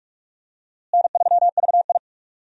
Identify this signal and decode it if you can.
Morse code, heard as a keyed tone, about 47 wpm, 700 Hz. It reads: N3VI
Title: USA Call Signs - 4 Characters - Repeated 2x - 45wpm